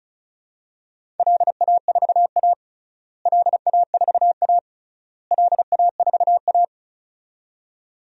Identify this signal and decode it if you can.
Morse code, heard as a keyed tone, about 34 wpm, 700 Hz. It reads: LA4A LA4A LA4A
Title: DXCC-100 Call Signs - 4 Characters - Repeated 3x - 35wpm